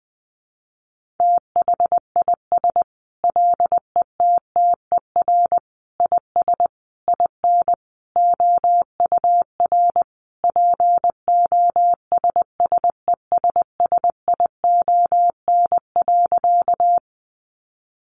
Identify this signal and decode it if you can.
Morse code, heard as a keyed tone, about 20 wpm, 700 Hz. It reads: THIS LETTER IS IN OUR POSSESSION.
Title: Sentences from Top 1400 Words 20wpm